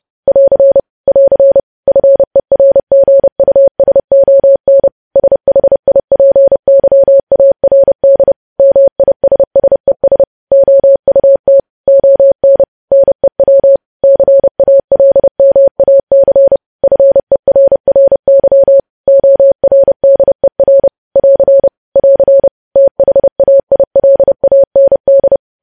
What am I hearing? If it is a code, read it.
Morse code, heard as a keyed tone, about 30 wpm, 550 Hz. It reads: <AR> <AR> FERGUSON SHIPYARD MISSES OUT ON NEW CALMAC FERRY ORDER <AR> <AR> THAILAND
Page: news headlines in morse code at 30 wpm